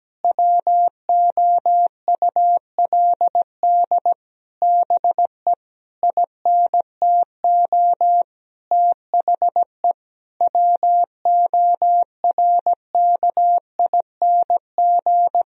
Morse code, heard as a keyed tone, about 17 wpm, 700 Hz. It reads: WOULD BE INTO THE WORKING
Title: Sentences from Top 1000 Words - Review Part 4 - 17wpm